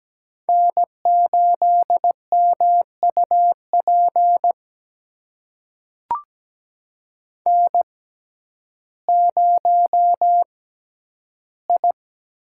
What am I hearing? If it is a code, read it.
Morse code, heard as a keyed tone, about 17 wpm, spaced as for 10 wpm, 700 Hz. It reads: N8MUP N0I
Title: USA Call Signs - 5 Characters - 6x Character Spacing - 17wpm